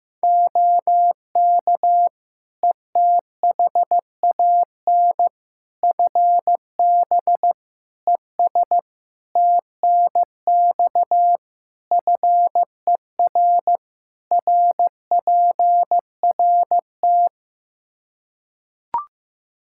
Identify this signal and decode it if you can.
Morse code, heard as a keyed tone, about 15 wpm, 700 Hz. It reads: OK ETHAN FB ES TNX FER RPRT
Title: Practice QSO - Exchange 2 - Info - 15wpm